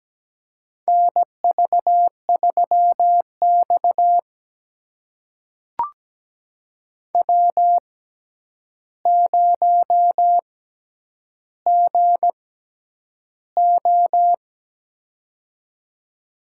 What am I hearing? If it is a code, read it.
Morse code, heard as a keyed tone, about 17 wpm, spaced as for 9 wpm, 700 Hz. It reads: NV3X W0GO
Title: USA Call Signs - 4 Characters - 6x Character Spacing - 17wpm